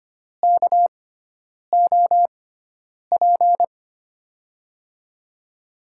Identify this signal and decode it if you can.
Morse code, heard as a keyed tone, about 25 wpm, spaced as for 9 wpm, 700 Hz. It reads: KOP
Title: USA Call Signs - 6x Character Spacing - 25wpm